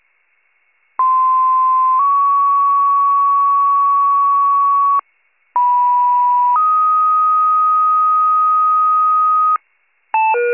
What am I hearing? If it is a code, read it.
Two-tone page, heard as a one-second tone, then a three-second tone, about 1040 then 1120 Hz, about 980 then 1250 Hz. It is heard from 1.0 to 9.6 s.